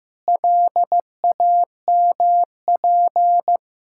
Morse code, heard as a keyed tone, about 16 wpm, 700 Hz. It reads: LAMP
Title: Pangram Challenge v2 - 15wpm